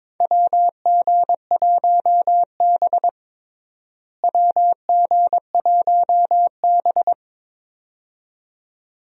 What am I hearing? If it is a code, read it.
Morse code, heard as a keyed tone, about 22 wpm, 700 Hz. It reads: WG1B WG1B
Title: USA Call Signs - 4 Characters - Repeated 2x - 22wpm